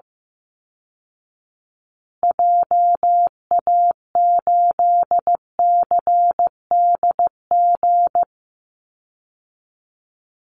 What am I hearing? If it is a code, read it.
Morse code, heard as a keyed tone, about 15 wpm, 700 Hz. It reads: JA8CDG
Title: Call Signs Repeated 3x - 15wpm